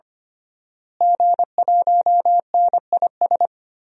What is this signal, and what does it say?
Morse code, heard as a keyed tone, about 25 wpm, 700 Hz. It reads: G1NIS
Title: DXCC-100 Call Signs - 5 Characters - Repeated 3x - 25wpm